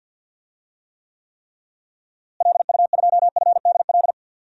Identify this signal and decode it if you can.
Morse code, heard as a keyed tone, about 52 wpm, 700 Hz. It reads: LU2FDL
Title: DXCC-100 Call Signs - 6 Characters - Repeated 3x - 50wpm